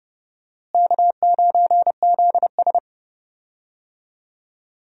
Morse code, heard as a keyed tone, about 30 wpm, 700 Hz. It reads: K9ZS
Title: USA Call Signs - 4 Characters - Repeated 3x - 30wpm